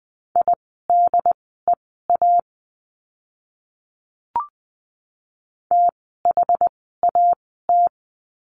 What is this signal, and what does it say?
Morse code, heard as a keyed tone, about 20 wpm, spaced as for 14 wpm, 700 Hz. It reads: IDEA THAT
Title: 108 Lesson 18f - Letter m - Words 20wpm